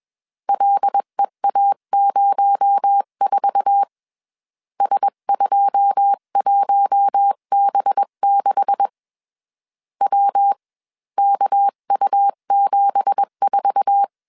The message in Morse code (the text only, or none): LEA04 S21B6 WKU74